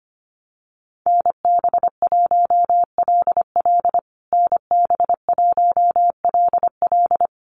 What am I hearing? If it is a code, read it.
Morse code, heard as a keyed tone, about 25 wpm, 700 Hz. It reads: NB1LL NB1LL
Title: USA Call Signs Repeated 3x 25wpm